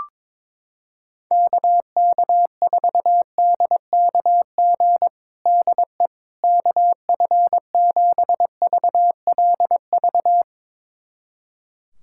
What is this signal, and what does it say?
Morse code, heard as a keyed tone, about 22 wpm, 700 Hz. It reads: KK4DKG DE KF7VLV
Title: USA Callsign DE Callsign 22wpm